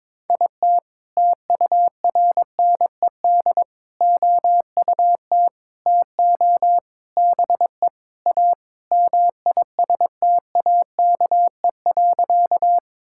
Morse code, heard as a keyed tone, about 22 wpm, 700 Hz. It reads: IT TURNED OUT TO BE A MISTAKE.
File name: practice / 22wpm / Sentences with 4-Word Phrases - 22wpm